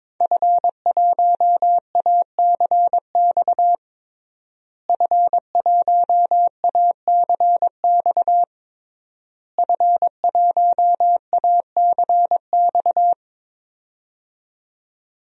Morse code, heard as a keyed tone, about 22 wpm, 700 Hz. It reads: F1ACX F1ACX F1ACX
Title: DXCC-100 Call Signs - 5 Characters - Repeated 3x - 22wpm